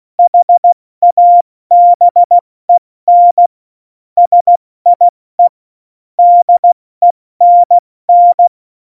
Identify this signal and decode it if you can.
Morse code, heard as a keyed tone, about 16 wpm, 700 Hz. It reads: HABEN SIE DENN
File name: Morsekurs / Witze / Witz 09. Juli 2019, 70 F80